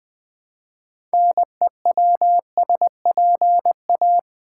Morse code, heard as a keyed tone, about 20 wpm, 700 Hz. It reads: NEWSPA
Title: Build A Word - 3 Letters At A Time - 9-Letter Words - 20wpm